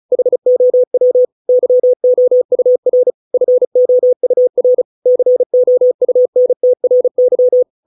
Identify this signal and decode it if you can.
Morse code, heard as a keyed tone, about 34 wpm, 500 Hz. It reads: HOW YOUR FOUR COUNTRY